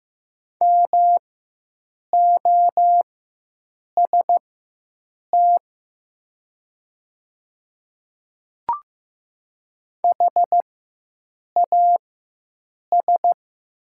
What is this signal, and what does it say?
Morse code, heard as a keyed tone, about 15 wpm, spaced as for 7 wpm, 700 Hz. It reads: MOST HAS
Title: 4x Character Spacing - Top 100 Words - 15wpm